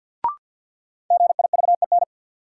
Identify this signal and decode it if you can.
Morse code, heard as a keyed tone, about 52 wpm, 700 Hz. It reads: GIVEN
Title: Sentences from Top 1400 Words - New Word Drill - 50wpm